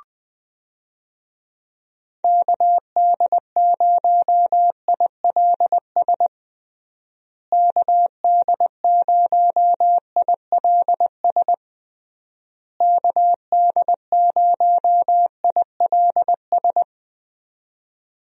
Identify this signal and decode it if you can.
Morse code, heard as a keyed tone, about 21 wpm, 700 Hz. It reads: KD0ILS KD0ILS KD0ILS
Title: USA Call Signs - 6 Characters - Repeated 3x - 20wpm